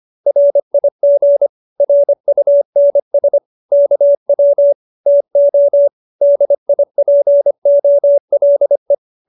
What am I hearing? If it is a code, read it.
Morse code, heard as a keyed tone, about 25 wpm, 570 Hz. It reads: RIG RUNS KW TO DIPOLE